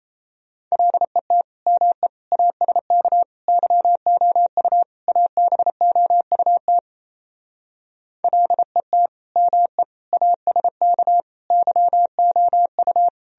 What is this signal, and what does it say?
Morse code, heard as a keyed tone, about 35 wpm, 700 Hz. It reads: LET ME ASK YOU ABOUT LET ME ASK YOU
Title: Speed-Racing - Sets of 5 Words 28wpm